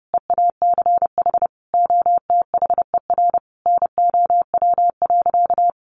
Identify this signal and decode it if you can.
Morse code, heard as a keyed tone, about 30 wpm, 700 Hz. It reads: EACH OTHER NOW.
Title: Sentences from Top 1200 Words 30wpm